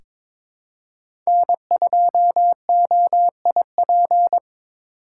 Morse code, heard as a keyed tone, about 22 wpm, 700 Hz. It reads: N2OIP
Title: USA Call Signs - 5 Characters - 2x Character Spacing - 22wpm